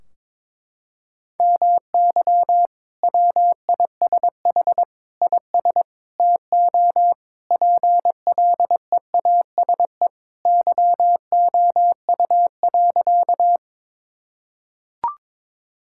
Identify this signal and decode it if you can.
Morse code, heard as a keyed tone, about 21 wpm, 700 Hz. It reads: MY WISH IS TO PLEASE YOU.
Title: Sentences from Top 1600 Words 22wpm